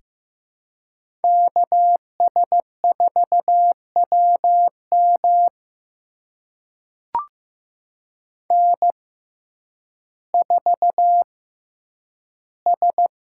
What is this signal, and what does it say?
Morse code, heard as a keyed tone, about 15 wpm, spaced as for 9 wpm, 700 Hz. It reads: KS4WM N4S
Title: USA Call Signs - 6x Character Spacing - 15wpm